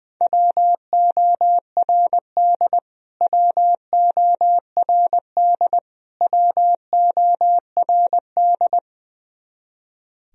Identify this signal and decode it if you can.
Morse code, heard as a keyed tone, about 21 wpm, 700 Hz. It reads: WORD WORD WORD
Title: Top 100 Words Repeated 20wpm